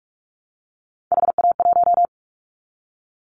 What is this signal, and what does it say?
Morse code, heard as a keyed tone, about 46 wpm, 700 Hz. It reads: HA1
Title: Callsign Prefixes for Top 25 DXCC - 45wpm